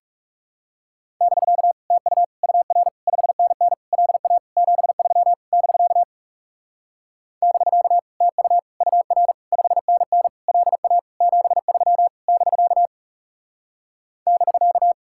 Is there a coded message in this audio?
Yes: Morse code from 1.2 to 14.9 s, heard as a keyed tone, about 40 wpm, 700 Hz. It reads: <BK> TU UR 5NN LA 73 <BK> <BK> TU UR 5NN LA 73 <BK> <BK>